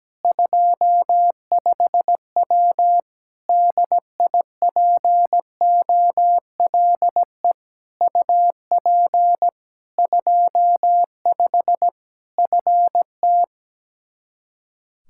Morse code, heard as a keyed tone, about 17 wpm, 700 Hz. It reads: 25W DIPOLE UP 25 FT
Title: K1USN SST QSO Practice 17wpm